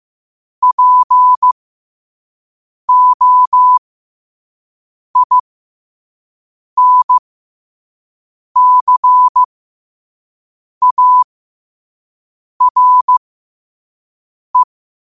Morse code, heard as a keyed tone, about 15 wpm, spaced as for 5 wpm, 1000 Hz. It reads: POINCARE
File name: qotdfiles / 5wpm